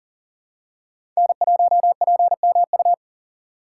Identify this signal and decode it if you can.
Morse code, heard as a keyed tone, about 40 wpm, 700 Hz. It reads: N1PMU
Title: USA Call Signs - 5 Characters - 2x Character Spacing - 40wpm